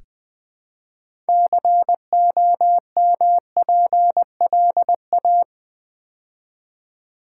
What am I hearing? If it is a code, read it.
Morse code, heard as a keyed tone, about 20 wpm, 700 Hz. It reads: COMPLA
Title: Build A Word - 3 Letters At A Time - 9-Letter Words - 20wpm